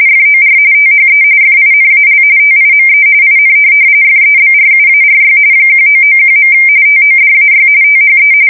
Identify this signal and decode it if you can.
RTTY, about 45.45 baud. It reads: FOR THE FINAL TIME IN FEBRUARY 1991. TODAY THE
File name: fsk170 / USS-New-Jersey